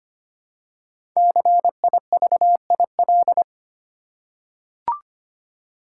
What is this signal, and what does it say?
Morse code, heard as a keyed tone, about 25 wpm, 700 Hz. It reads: CIVIL
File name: 5 Letter Words - First 3 Letters Given - 25wpm